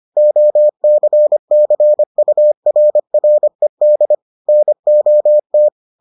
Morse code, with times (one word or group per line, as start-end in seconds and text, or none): 0.2-4.1 OCCURRED
4.5-5.7 NOT